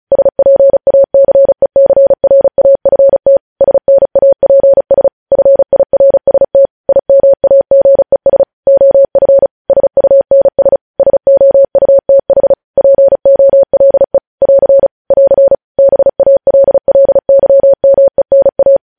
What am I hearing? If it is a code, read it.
Morse code, heard as a keyed tone, about 36 wpm, 550 Hz. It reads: SPACECRAFT SNAPS FIRST IMAGES OF SUNS SOUTH POLE <AR> <AR> BALLYMENA
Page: news headlines in morse code at 35 wpm